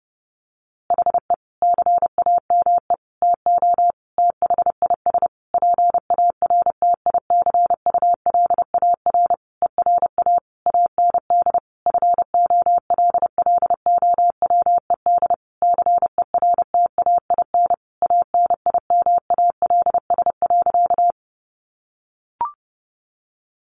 Morse code, heard as a keyed tone, about 30 wpm, 700 Hz. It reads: HE CAME TO THIS PARTICULAR ERA AND FOLLOWED CERTAIN ANIMALS.
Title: A Sound Of Thunder 30wpm